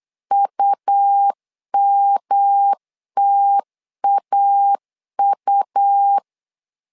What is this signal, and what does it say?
Morse code, heard as a keyed tone, about 8 wpm, 800 Hz. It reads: UMTAU
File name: LEZIONI / 3 MWRO / 3 M 8wpm 2300